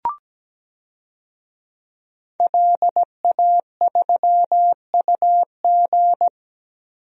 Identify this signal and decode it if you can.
Morse code, heard as a keyed tone, about 17 wpm, 700 Hz. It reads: LA3UG
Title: DXCC-100 Call Signs - 5 Characters - Repeated 2x - 17wpm